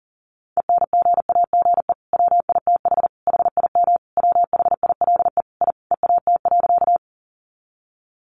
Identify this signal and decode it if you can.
Morse code, heard as a keyed tone, about 40 wpm, 700 Hz. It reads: ENGAGE WITH HIM WHILE I EAT.
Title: Sentences from Top 1500 Words 40wpm